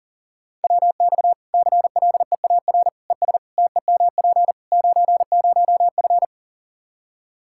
Morse code, heard as a keyed tone, about 40 wpm, 700 Hz. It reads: WX CLEAR ES TEMP 90F
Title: Practice QSO - Exchange 2 - Info - 40wpm